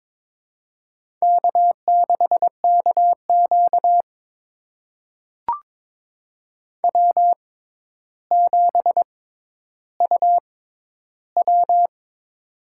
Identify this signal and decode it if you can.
Morse code, heard as a keyed tone, about 22 wpm, spaced as for 11 wpm, 700 Hz. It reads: K6KQ W7UW
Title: USA Call Signs - 4 Characters - 6x Character Spacing - 22wpm